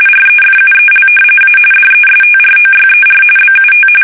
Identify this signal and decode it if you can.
RTTY, about 45.45 baud. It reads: REFUEL A DESTROYER BUT
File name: fsk850 / battleshipnj